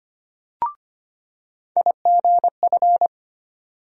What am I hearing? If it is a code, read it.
Morse code, heard as a keyed tone, about 25 wpm, 700 Hz. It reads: IGF
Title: USA Call Sign Postfixes - 3 Letters 25wpm